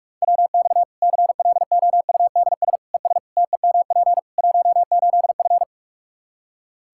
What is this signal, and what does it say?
Morse code, heard as a keyed tone, about 46 wpm, 700 Hz. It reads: WX CLOUDS ES TEMP 18F
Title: Practice QSO - Exchange 2 - Info - 45wpm